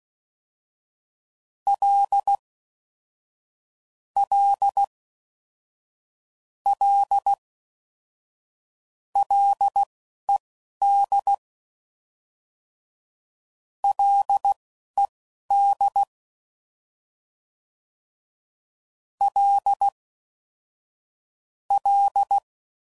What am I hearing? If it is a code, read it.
Morse code, heard as a keyed tone, about 16 wpm, spaced as for 11 wpm, 800 Hz. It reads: L L L LED LED L L
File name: Prefixes / FLs & THs / L Prefixes